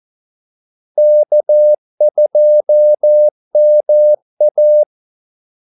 Morse code, heard as a keyed tone, about 14 wpm, 600 Hz. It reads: K2MA
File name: Callsigns-14wpm